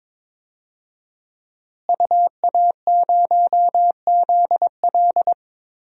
Morse code, heard as a keyed tone, about 22 wpm, 700 Hz. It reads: UA0ZL
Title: DXCC-100 Call Signs - 5 Characters - Repeated 3x - 22wpm